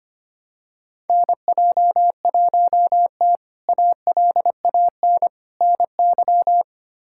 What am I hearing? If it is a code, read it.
Morse code, heard as a keyed tone, about 25 wpm, 700 Hz. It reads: NJ1T ALAN NY
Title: K1USN SST QSO Practice - 25wpm